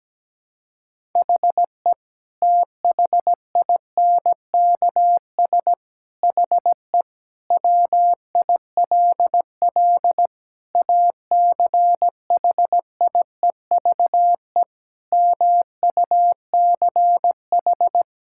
Morse code, heard as a keyed tone, about 17 wpm, 700 Hz. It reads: HE THINKS HE WILL ACHIEVE MUCH
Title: Sentences from Top 1000 Words - Review Part 6 - 17wpm